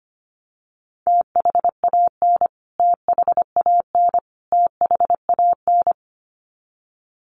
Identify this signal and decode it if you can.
Morse code, heard as a keyed tone, about 25 wpm, 700 Hz. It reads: THAN THAN THAN
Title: Top 300 Words Repeated 25wpm